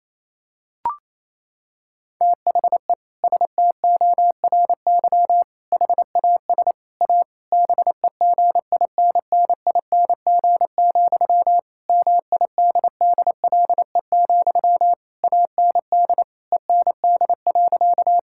Morse code, heard as a keyed tone, about 28 wpm, 700 Hz. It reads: THE STORY HAS A BEGINNING, MIDDLE, AND END.
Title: Sentences from Top 1000 Words - Review Part 6 - 28wpm